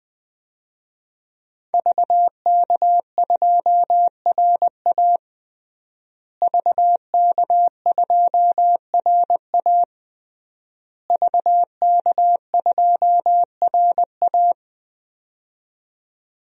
Morse code, heard as a keyed tone, about 21 wpm, 700 Hz. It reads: VK2RA VK2RA VK2RA
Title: DXCC-100 Call Signs - 5 Characters - Repeated 3x - 20wpm